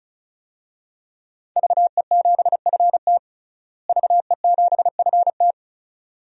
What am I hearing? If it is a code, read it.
Morse code, heard as a keyed tone, about 36 wpm, 700 Hz. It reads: VE7FT VE7FT
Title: DXCC-100 Call Signs - 5 Characters - Repeated 2x - 35wpm